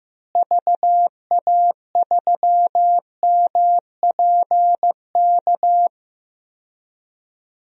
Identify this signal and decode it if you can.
Morse code, heard as a keyed tone, about 16 wpm, 700 Hz. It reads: VA3MPK
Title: DXCC-100 Call Signs - 6 Characters - Repeated 2x - 15wpm